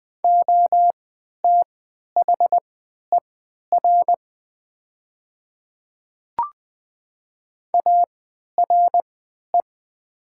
Morse code, heard as a keyed tone, about 20 wpm, spaced as for 11 wpm, 700 Hz. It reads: OTHER ARE
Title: 3x Character Spacing - Top 100 Words - 20wpm